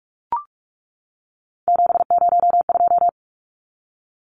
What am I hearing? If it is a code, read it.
Morse code, heard as a keyed tone, about 44 wpm, 700 Hz. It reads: 702
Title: Numbers - 3-Digit 45wpm